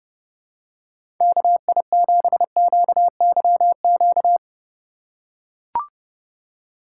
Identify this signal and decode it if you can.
Morse code, heard as a keyed tone, about 32 wpm, 700 Hz. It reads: KI7QYQ
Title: USA Call Signs - 6 Characters - 6x Character Spacing - 30wpm